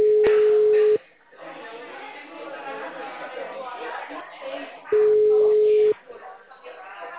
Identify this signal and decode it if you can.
Ringback tone (Europe, 425 Hz), with speech.
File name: out- -2006- - - .62410